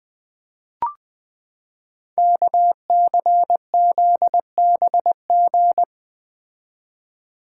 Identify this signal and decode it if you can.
Morse code, heard as a keyed tone, about 20 wpm, 700 Hz. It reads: KCZBG
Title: Code Groups - 5 Letters - 20wpm